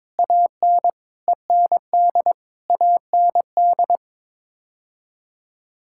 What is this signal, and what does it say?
Morse code, heard as a keyed tone, about 22 wpm, 700 Hz. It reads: AN END AND
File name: Sets of 9 Words - 22wpm